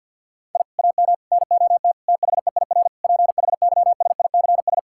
Morse code, heard as a keyed tone, about 50 wpm, 700 Hz. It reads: I AM NOT THEIR PHYSICS
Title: Sentences from Top 1000 Words - Review Part 6 - 50wpm